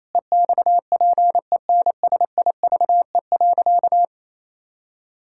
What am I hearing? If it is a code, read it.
Morse code, heard as a keyed tone, about 28 wpm, 700 Hz. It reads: EXPENSIVE.
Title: Pangram Challenge v2 - 28wpm